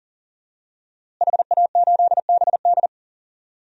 Morse code, heard as a keyed tone, about 40 wpm, 700 Hz. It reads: HA8BD
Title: DXCC-100 Call Signs - 5 Characters - Repeated 2x - 40wpm